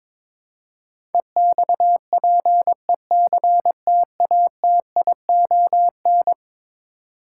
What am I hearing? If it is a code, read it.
Morse code, heard as a keyed tone, about 22 wpm, 700 Hz. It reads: EXPECTATION
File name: practice / 22wpm / Eleven Letter Words - 22wpm